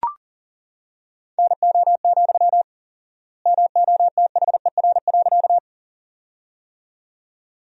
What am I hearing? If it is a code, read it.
Morse code, heard as a keyed tone, about 40 wpm, 700 Hz. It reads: NO, MOTHER.
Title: 4x Word-Spacing - Sentences from Top 200 Words - 40wpm